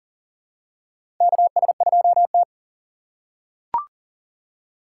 Morse code, heard as a keyed tone, about 40 wpm, 700 Hz. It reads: KS2T
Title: USA Call Signs - 6x Character Spacing - 40wpm